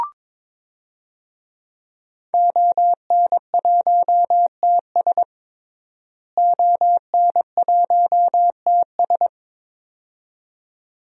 Morse code, heard as a keyed tone, about 22 wpm, 700 Hz. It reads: ON1TS ON1TS
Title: DXCC-100 Call Signs - 5 Characters - Repeated 2x - 22wpm